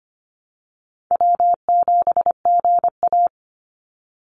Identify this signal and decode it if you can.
Morse code, heard as a keyed tone, about 25 wpm, 700 Hz. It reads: W7GA
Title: NA SOTA Chaser Callsigns 25wpm